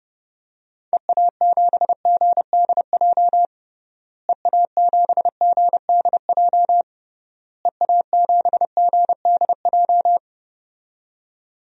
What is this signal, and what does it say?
Morse code, heard as a keyed tone, about 30 wpm, 700 Hz. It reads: EA7GDJ EA7GDJ EA7GDJ
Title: DXCC-100 Call Signs - 6 Characters - Repeated 3x - 30wpm